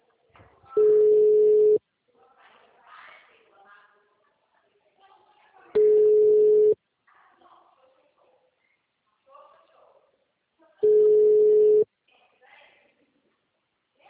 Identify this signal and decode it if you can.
Ringback tone (Europe, 425 Hz), with speech.
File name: out- -2012- - - .65712